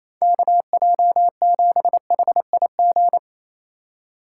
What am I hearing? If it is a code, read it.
Morse code, heard as a keyed tone, about 28 wpm, 700 Hz. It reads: KJ7HIG
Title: USA Call Signs - 3x Character Spacing - 28wpm